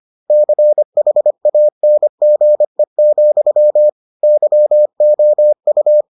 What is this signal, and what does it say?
Morse code, heard as a keyed tone, about 25 wpm, 600 Hz. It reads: CHANGE, YOU